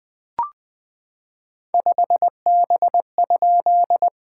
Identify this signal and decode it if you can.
Morse code, heard as a keyed tone, about 19 wpm, 700 Hz. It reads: 5B?
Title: SOTA QSO Practice 20wpm